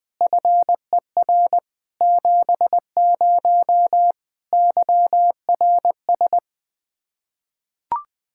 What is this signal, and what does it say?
Morse code, heard as a keyed tone, about 20 wpm, 700 Hz. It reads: FER 70 YRS
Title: Practice QSO - Exchange 3 - More Info - 20wpm